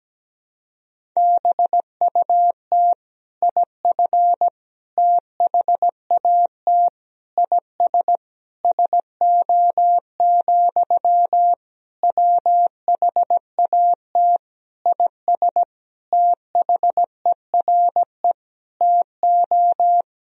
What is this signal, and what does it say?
Morse code, heard as a keyed tone, about 17 wpm, 700 Hz. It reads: BUT IF THAT IS SO, WHAT IS THERE TO
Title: Sentences from Top 200 Words and Abbreviations 17wpm